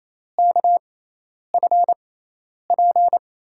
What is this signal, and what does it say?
Morse code, heard as a keyed tone, about 28 wpm, spaced as for 10 wpm, 700 Hz. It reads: KFP
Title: USA Call Signs - 5 Characters - 6x Character Spacing - 28wpm